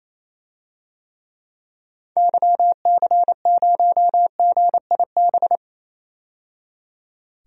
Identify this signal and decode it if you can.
Morse code, heard as a keyed tone, about 28 wpm, 700 Hz. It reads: YC0GIB
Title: DXCC-100 Call Signs - 6 Characters - 28wpm